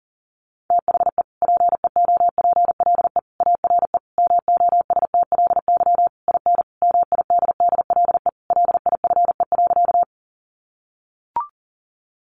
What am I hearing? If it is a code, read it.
Morse code, heard as a keyed tone, about 40 wpm, 700 Hz. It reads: THE PEOPLE ARE MOSTLY IN MIDDLE LIFE.
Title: Sentences from Top 1300 Words 40wpm